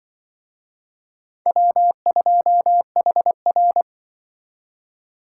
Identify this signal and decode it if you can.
Morse code, heard as a keyed tone, about 24 wpm, 700 Hz. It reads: W2HR
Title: Speed Racing - USA 1x2 Call Signs 20wpm